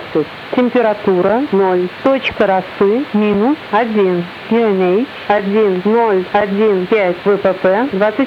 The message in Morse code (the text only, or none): none